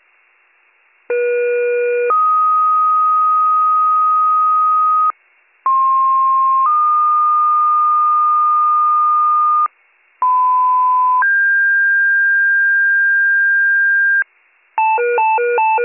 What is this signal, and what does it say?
Two-tone page, heard as a one-second tone, then a three-second tone, about 480 then 1250 Hz, about 1060 then 1220 Hz, about 1010 then 1600 Hz. It is heard from 1.1 to 14.2 s.